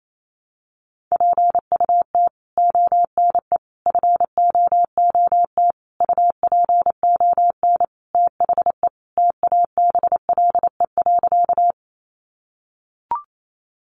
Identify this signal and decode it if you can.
Morse code, heard as a keyed tone, about 28 wpm, 700 Hz. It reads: PUT ONE FOOT UPON THE TABLE.
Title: Sentences from Top 1100 Words 28wpm